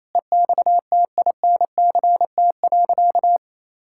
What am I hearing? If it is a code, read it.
Morse code, heard as a keyed tone, about 27 wpm, 700 Hz. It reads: EXTINCT.
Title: Pangram Challenge v2 - 28wpm